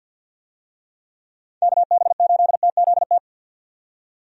Morse code, heard as a keyed tone, about 51 wpm, 700 Hz. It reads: KB8TZT
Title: USA Call Signs - 6 Characters - 50wpm